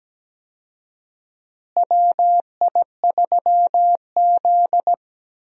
Morse code, heard as a keyed tone, about 17 wpm, 700 Hz. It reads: WI3Z
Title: USA Call Signs - 4 Characters - Repeated 3x - 17wpm